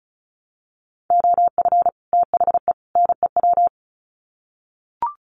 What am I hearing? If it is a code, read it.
Morse code, heard as a keyed tone, about 34 wpm, 700 Hz. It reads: OF THE NEW
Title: Sets of 3 Words 35wpm